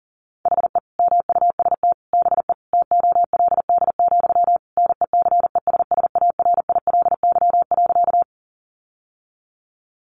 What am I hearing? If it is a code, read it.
Morse code, heard as a keyed tone, about 40 wpm, 700 Hz. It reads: HE MUST BE TOLD, NECESSARILY.
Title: Sentences from Top 1400 Words 40wpm